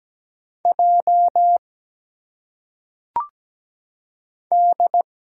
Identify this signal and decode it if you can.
Morse code, heard as a keyed tone, about 17 wpm, 700 Hz. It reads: J D
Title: Single Letters - Phonetic Pronunciation 17wpm